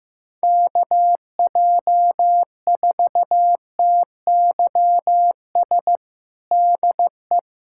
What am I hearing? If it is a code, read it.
Morse code, heard as a keyed tone, about 15 wpm, 700 Hz. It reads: KJ4TYS DE
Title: USA Callsign DE Callsign - 6 Characters - 15wpm